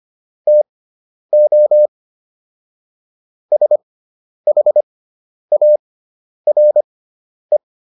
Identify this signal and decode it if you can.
Morse code, heard as a keyed tone, about 25 wpm, spaced as for 10 wpm, 600 Hz. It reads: TO SHARE